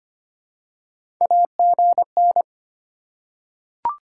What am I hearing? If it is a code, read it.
Morse code, heard as a keyed tone, about 25 wpm, 700 Hz. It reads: AGN
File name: Top 90 QSO Elements - 3x Character Spacing - 25wpm